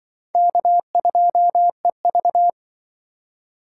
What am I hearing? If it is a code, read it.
Morse code, heard as a keyed tone, about 26 wpm, 700 Hz. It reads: K2EV